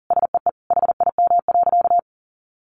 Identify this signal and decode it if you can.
Morse code, heard as a keyed tone, about 40 wpm, 700 Hz. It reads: SEE HIM.
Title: Sentences from Top 1500 Words 40wpm